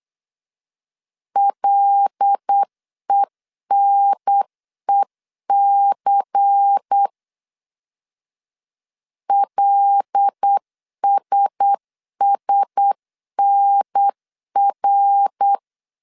Morse code, heard as a keyed tone, about 8 wpm, 800 Hz. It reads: LENEC LSSNR
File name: CW / DET / 1 6 1500C S C 8wpm 5219